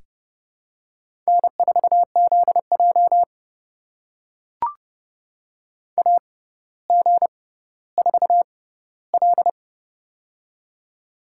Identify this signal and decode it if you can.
Morse code, heard as a keyed tone, about 30 wpm, spaced as for 15 wpm, 690 Hz. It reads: N4ZJ AG4L
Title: USA Call Signs - 4 Characters - 6x Character Spacing - 30wpm